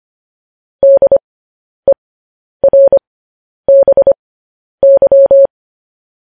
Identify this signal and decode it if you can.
Morse code, heard as a keyed tone, about 25 wpm, spaced as for 10 wpm, 550 Hz. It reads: DERBY
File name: podcast / News headlines-1x-25x10